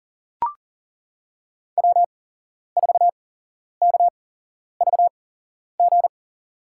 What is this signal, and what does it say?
Morse code, heard as a keyed tone, about 40 wpm, spaced as for 11 wpm, 700 Hz. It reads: W4KVG
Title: USA Call Signs - 8x Character Spacing - 40wpm